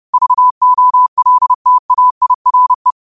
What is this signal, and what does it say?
Morse code, heard as a keyed tone, about 30 wpm, 1000 Hz. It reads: VOLTAIRE